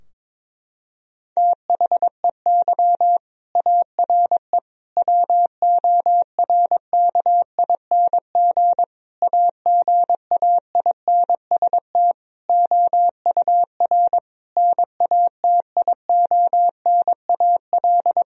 Morse code, heard as a keyed tone, about 22 wpm, 700 Hz. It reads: THEY ARE WORKING AGAINST OUR NATIONAL
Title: Sentences from Top 1000 Words - Review Part 4 - 22wpm